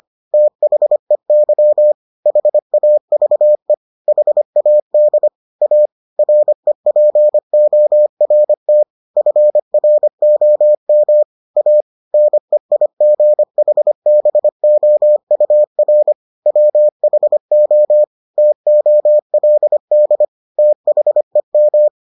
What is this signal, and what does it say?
Morse code, heard as a keyed tone, about 25 wpm, 600 Hz. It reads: THEY HAVE HAD A REPORT FROM A NEIGHBOUR WHO TOLD THEM